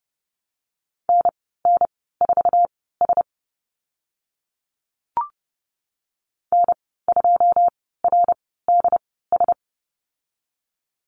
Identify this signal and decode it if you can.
Morse code, heard as a keyed tone, about 30 wpm, spaced as for 17 wpm, 700 Hz. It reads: NN4S N2RDS
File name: Lesson 15g - Number 5 - Call Signs 30wpm